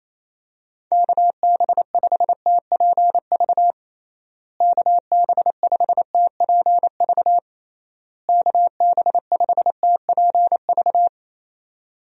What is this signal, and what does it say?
Morse code, heard as a keyed tone, about 28 wpm, 700 Hz. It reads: KB5TPV KB5TPV KB5TPV
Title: USA Call Signs - 6 Characters - Repeated 3x - 28wpm